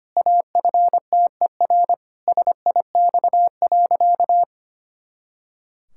Morse code, heard as a keyed tone, about 25 wpm, 700 Hz. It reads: AFTER SIX.
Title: Pangram Challenge v2 - 25wpm